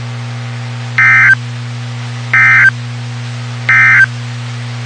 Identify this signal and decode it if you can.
Emergency Alert System (SAME) tones, heard as harsh buzzing bursts. It is heard from 0.9 to 4.1 s.
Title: WXK91 NOAA Weather Radio